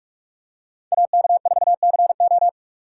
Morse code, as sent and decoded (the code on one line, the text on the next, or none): .- -.- ....- -.-. ---
AK4CO